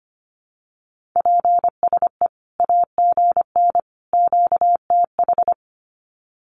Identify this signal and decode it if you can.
Morse code, heard as a keyed tone, about 25 wpm, 700 Hz. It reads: PSE AGN QTH
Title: Common QSO Phrases 25wpm